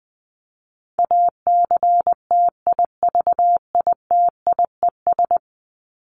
Morse code, heard as a keyed tone, about 19 wpm, 700 Hz. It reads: ACTIVITIES
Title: ES:S Suffix - Top 100 Words with Base Words 20wpm